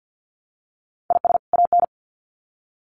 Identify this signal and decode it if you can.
Morse code, heard as a keyed tone, about 50 wpm, 700 Hz. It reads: IS AN
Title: Sets of 2 Words 50wpm